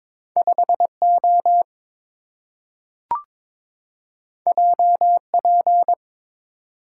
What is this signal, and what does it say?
Morse code, heard as a keyed tone, about 22 wpm, 700 Hz. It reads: 5O JP
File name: practice / 22wpm / Two Letter-Number Combinations - 22wpm